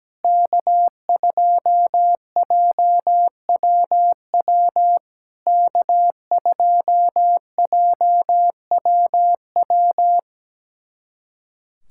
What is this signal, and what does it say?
Morse code, heard as a keyed tone, about 17 wpm, 700 Hz. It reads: K2JWW K2JWW
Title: SOTA QSO Practice 17wpm